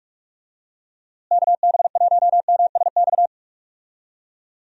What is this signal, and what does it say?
Morse code, heard as a keyed tone, about 43 wpm, 700 Hz. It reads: KB1MSX
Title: USA Call Signs - 6 Characters - Repeated 3x - 45wpm